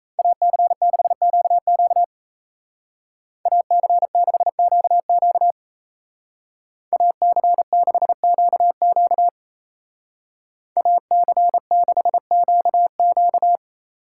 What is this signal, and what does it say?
Morse code, heard as a keyed tone, about 37 wpm, 700 Hz. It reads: AC6QQ AC6QQ AC6QQ AC6QQ